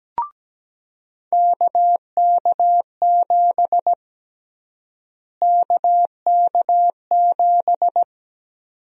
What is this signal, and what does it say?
Morse code, heard as a keyed tone, about 17 wpm, 700 Hz. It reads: KK7 KK7
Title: USA Call Sign Prefixes - Repeated 2x - 17wpm